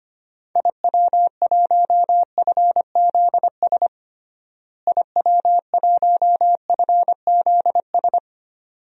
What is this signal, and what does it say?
Morse code, heard as a keyed tone, about 25 wpm, 700 Hz. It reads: IW1FZS IW1FZS
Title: DXCC-100 Call Signs - 6 Characters - Repeated 2x - 25wpm